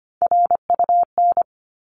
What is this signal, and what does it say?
Morse code, heard as a keyed tone, about 25 wpm, 700 Hz. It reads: RUN
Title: Top 300 Words 25wpm